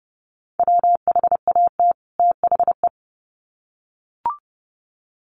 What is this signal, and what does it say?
Morse code, heard as a keyed tone, about 30 wpm, 700 Hz. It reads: WHAT THE
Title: Sets of 2 Words 30wpm